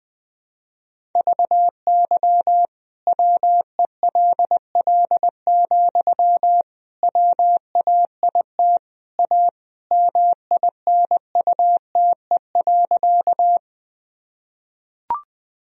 Morse code, heard as a keyed tone, about 20 wpm, 700 Hz. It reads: VY WELL, WAIT A MINUTE.
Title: Sentences from Top 500 Words and Abbreviations 20wpm